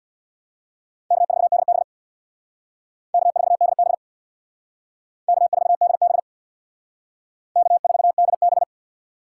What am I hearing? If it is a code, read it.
Morse code, heard as a keyed tone, about 61 wpm, 700 Hz. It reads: K4DB K4DB K4DB K4DB
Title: Speed Racing - USA 1x2 Call Signs 50wpm